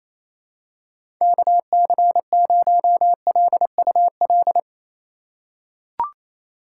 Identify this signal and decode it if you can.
Morse code, heard as a keyed tone, about 28 wpm, 700 Hz. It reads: KC0LUL
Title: USA Call Signs - 2x Character Spacing - 28wpm